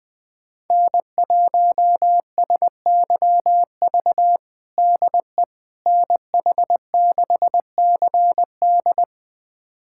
Morse code, heard as a keyed tone, about 20 wpm, 700 Hz. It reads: N1SYV DE NH6CD
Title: USA Callsign DE Callsign - 5 Characters - 20wpm